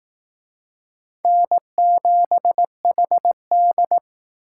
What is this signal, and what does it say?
Morse code, heard as a keyed tone, about 18 wpm, 700 Hz. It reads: N7HD
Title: Speed Racing - USA 1x2 Call Signs 15wpm